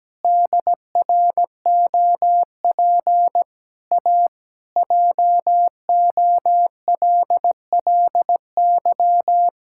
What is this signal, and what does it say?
Morse code, heard as a keyed tone, about 17 wpm, 700 Hz. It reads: DROP A JOLLY
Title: Pangram Challenge v2 - 17wpm